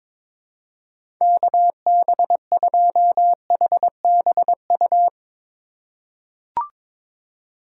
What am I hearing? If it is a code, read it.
Morse code, heard as a keyed tone, about 22 wpm, 700 Hz. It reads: KB2HBU
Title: USA Call Signs - 6 Characters - 2x Character Spacing - 22wpm